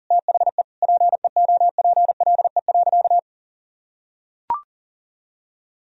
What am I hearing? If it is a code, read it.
Morse code, heard as a keyed tone, about 40 wpm, 700 Hz. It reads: THE PEOPLE.
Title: Sentences from Top 1000 Words - Review Part 3 - 40wpm